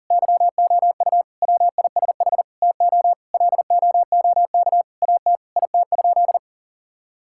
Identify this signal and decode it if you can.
Morse code, heard as a keyed tone, about 40 wpm, 700 Hz. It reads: YOU WISH TO LOOK AT IT?
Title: Sentences from Top 1600 Words 40wpm